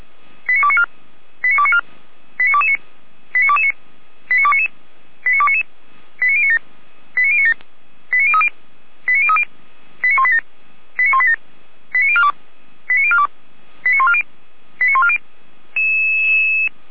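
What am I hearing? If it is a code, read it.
Five-tone selective call (ZVEI), heard as quick runs of five tones, tones of about 70 ms. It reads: 89294 89294 89229 89229 89299 89299 89097 89097 89030 89030 89187 89187 89042 89042 89140 89140
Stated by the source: ZVEI five-tone call received by SDR, from the feuerwehr-orlamuende.org audio server.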